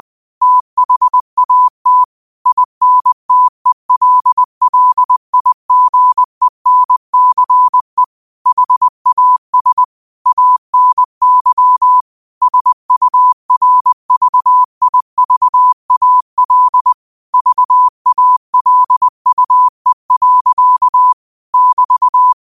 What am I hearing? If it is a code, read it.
Morse code, heard as a keyed tone, about 20 wpm, 1000 Hz. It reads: THAT INTELLIGENCE HAS ANY SURVIVAL VALUE. <BT>
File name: qotdfiles / 20wpm